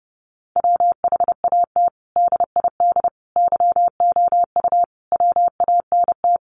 Morse code, heard as a keyed tone, about 30 wpm, 700 Hz. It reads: WHAT DID YOU WANT
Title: Sentences from Top 1400 Words 30wpm